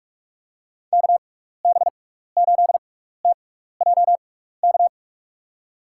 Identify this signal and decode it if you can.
Morse code, heard as a keyed tone, about 45 wpm, spaced as for 15 wpm, 700 Hz. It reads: KB8TJK
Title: USA Call Signs - 6 Characters - 6x Character Spacing - 45wpm